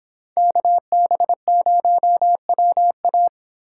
Morse code, heard as a keyed tone, about 25 wpm, 700 Hz. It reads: KB0WA